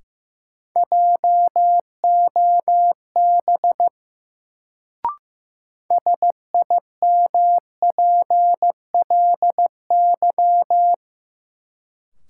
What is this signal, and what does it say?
Morse code, heard as a keyed tone, about 15 wpm, 700 Hz. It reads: JOB SIMPLY
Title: Sentences from Top 600 Words - New Word Drill - 15wpm